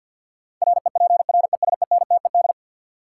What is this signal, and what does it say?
Morse code, heard as a keyed tone, about 50 wpm, 700 Hz. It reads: REPRESENTED
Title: Sets of 5 Words - New Word Drill - 50wpm